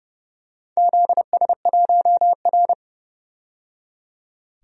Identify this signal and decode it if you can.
Morse code, heard as a keyed tone, about 30 wpm, 700 Hz. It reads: ZS1R
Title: DXCC-100 Call Signs - 4 Characters - 30wpm